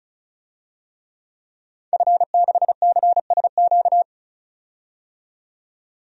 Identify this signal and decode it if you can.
Morse code, heard as a keyed tone, about 36 wpm, 700 Hz. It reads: F6CSQ
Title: DXCC-100 Call Signs - 5 Characters - 35wpm